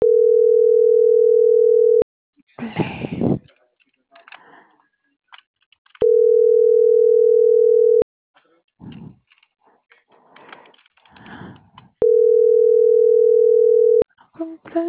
Ringback tone (North America), 440 + 480 Hz.